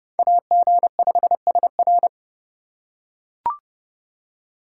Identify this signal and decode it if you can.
Morse code, heard as a keyed tone, about 30 wpm, 700 Hz. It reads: AG5SR